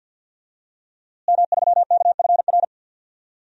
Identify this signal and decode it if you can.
Morse code, heard as a keyed tone, about 50 wpm, 700 Hz. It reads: M3KFR
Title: DXCC-100 Call Signs - 5 Characters - Repeated 2x - 50wpm